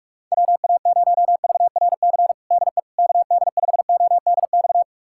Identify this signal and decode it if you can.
Morse code, heard as a keyed tone, about 41 wpm, 700 Hz. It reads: WA0VRC DE KD5ODX
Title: USA Callsign DE Callsign - 6 Characters - 45wpm